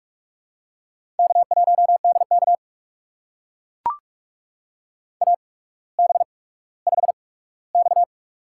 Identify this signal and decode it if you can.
Morse code, heard as a keyed tone, about 44 wpm, spaced as for 19 wpm, 700 Hz. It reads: K1DK AB5X
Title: USA Call Signs - 4 Characters - 8x Character Spacing - 45wpm